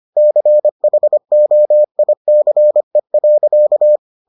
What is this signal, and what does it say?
Morse code, heard as a keyed tone, about 25 wpm, 600 Hz. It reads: CHOICE.